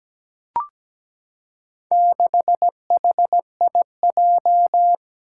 Morse code, heard as a keyed tone, about 17 wpm, 700 Hz. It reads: 6HIJ
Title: Four Challenging Letter-Number Combinations 17wpm